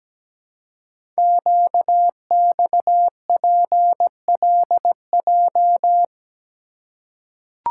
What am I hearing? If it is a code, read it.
Morse code, heard as a keyed tone, about 17 wpm, 700 Hz. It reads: QXPLJ
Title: Code Groups - 5 Letters - 17wpm